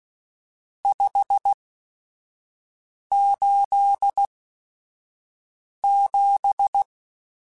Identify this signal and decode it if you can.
Morse code, heard as a keyed tone, about 16 wpm, spaced as for 5 wpm, 800 Hz. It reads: 587
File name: RSTs / RST-mp3s / Various RST @5